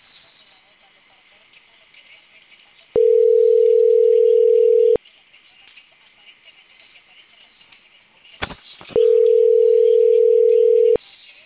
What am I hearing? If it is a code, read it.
Ringback tone (North America), 440 + 480 Hz, with speech.
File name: CENTRAL CAEA / out agcaea02 - .39629